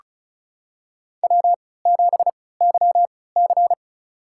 Morse code, heard as a keyed tone, about 36 wpm, spaced as for 20 wpm, 700 Hz. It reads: W7YC
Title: USA Call Signs - 4 Characters - 3x Character Spacing - 35wpm